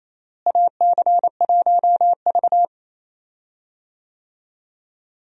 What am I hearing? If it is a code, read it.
Morse code, heard as a keyed tone, about 28 wpm, 700 Hz. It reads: AC1V
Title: USA Call Signs - 4 Characters - Repeated 3x - 28wpm